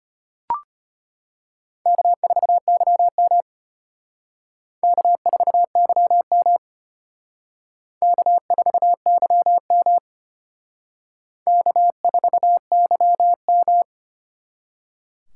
Morse code, heard as a keyed tone, about 31 wpm, 700 Hz. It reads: K4YM K4YM K4YM K4YM
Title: Speed Racing - USA 1x2 Call Signs 25wpm